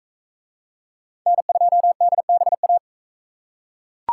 Morse code, heard as a keyed tone, about 41 wpm, 700 Hz. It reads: N2DBA